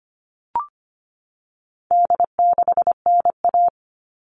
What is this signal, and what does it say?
Morse code, heard as a keyed tone, about 25 wpm, 700 Hz. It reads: D6NA
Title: All Cut Numbers - 4-Digit 25wpm